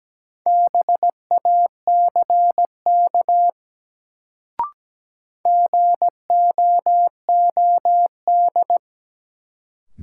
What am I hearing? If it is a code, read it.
Morse code, heard as a keyed tone, about 17 wpm, 700 Hz. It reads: BACK GOOD
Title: Sentences from Top 200 Words - New Word Drill - 17wpm